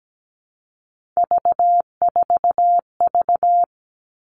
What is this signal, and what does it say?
Morse code, heard as a keyed tone, about 17 wpm, 700 Hz. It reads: V4V
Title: All Cut Numbers - 3-Digit 17wpm